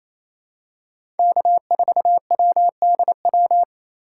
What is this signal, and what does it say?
Morse code, heard as a keyed tone, about 28 wpm, 700 Hz. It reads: K4WDW
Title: USA Call Signs - 4x Character Spacing - 28wpm